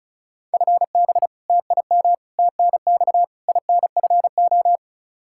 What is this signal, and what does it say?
Morse code, heard as a keyed tone, about 36 wpm, 700 Hz. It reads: FB TIM TNX INFO
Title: K1USN SST QSO Practice 35wpm